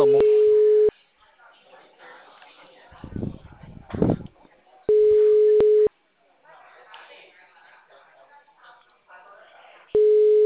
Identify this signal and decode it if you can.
Ringback tone (Europe, 425 Hz), with speech.